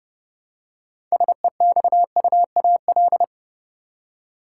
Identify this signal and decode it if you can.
Morse code, heard as a keyed tone, about 29 wpm, 700 Hz. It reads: SEXUAL
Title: Six Letter Words - 30wpm